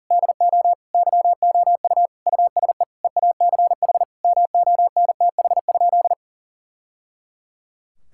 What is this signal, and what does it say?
Morse code, heard as a keyed tone, about 40 wpm, 700 Hz. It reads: DO YOU USE EACH MONTH?
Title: Sentences from Top 1000 Words - Review Part 6 - 40wpm